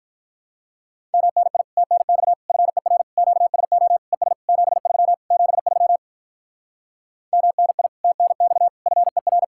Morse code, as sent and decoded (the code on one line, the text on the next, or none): -- -. .. / - -. -..- / ..-. .-. / --.- ... --- / . ... / --... ...-- / --... ...-- / -- -. .. / - -. -..- / ..-. .-.
MNI TNX FER QSO ES 73 73 MNI TNX FER